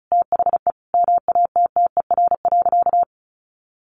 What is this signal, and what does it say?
Morse code, heard as a keyed tone, about 34 wpm, 700 Hz. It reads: THE MATTER.
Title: Sentences from Top 1300 Words 35wpm